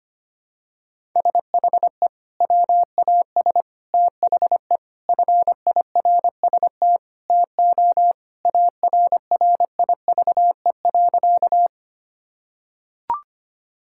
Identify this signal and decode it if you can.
Morse code, heard as a keyed tone, about 25 wpm, 700 Hz. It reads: SHE WAS THE FIRST TO ARRIVE.
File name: practice / 25wpm / Sentences with 3-Word Phrases - 25wpm